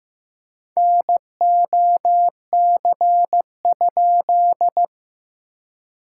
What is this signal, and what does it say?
Morse code, heard as a keyed tone, about 15 wpm, 700 Hz. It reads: NOC?
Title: SOTA QSO Practice 15wpm